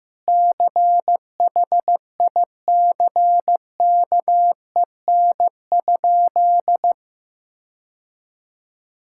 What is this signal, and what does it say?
Morse code, heard as a keyed tone, about 16 wpm, 700 Hz. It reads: CHICKEN?
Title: Sentences from Top 1000 Words - Review Part 5 - 15wpm